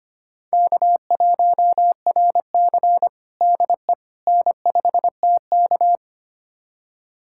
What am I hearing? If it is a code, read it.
Morse code, heard as a keyed tone, about 25 wpm, 700 Hz. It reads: K1RC DE N5TK
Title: USA Callsign DE Callsign - 4 Characters - 25wpm